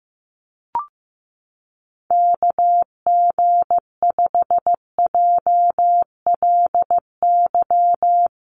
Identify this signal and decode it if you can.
Morse code, heard as a keyed tone, about 15 wpm, 700 Hz. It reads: KG5JLY
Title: USA Call Signs 15wpm